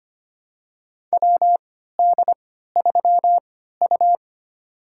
Morse code, heard as a keyed tone, about 25 wpm, spaced as for 14 wpm, 700 Hz. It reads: WD3U
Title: USA Call Signs - 3x Character Spacing - 25wpm